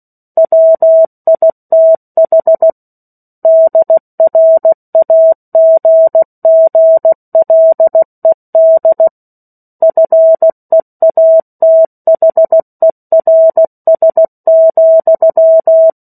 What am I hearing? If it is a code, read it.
Morse code, heard as a keyed tone, about 16 wpm, 650 Hz. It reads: WITH DRAGGLED FEATHERS,
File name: 03-Chapter-03 16 WPM